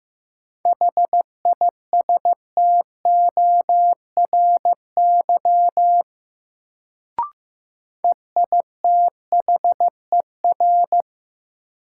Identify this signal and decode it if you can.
Morse code, heard as a keyed tone, about 15 wpm, 700 Hz. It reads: HISTORY EITHER
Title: Sentences from Top 600 Words - New Word Drill - 15wpm